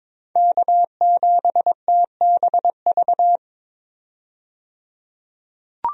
Morse code, heard as a keyed tone, about 21 wpm, 700 Hz. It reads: K7TBV
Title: USA Call Signs - 5 Characters - 22wpm